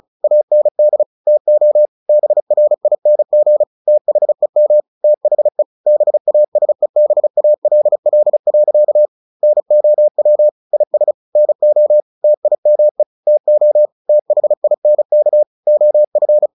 Morse code, heard as a keyed tone, about 34 wpm, 600 Hz. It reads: AND TO BRING THEM THE BASEBALL. NOW IS NO TIME TO THINK OF
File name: the old man and the sea / the old man and the sea 1 35wpm